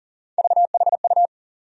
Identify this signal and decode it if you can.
Morse code, heard as a keyed tone, about 40 wpm, 700 Hz. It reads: VHU
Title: USA Call Sign Postfixes - 3 Letters 40wpm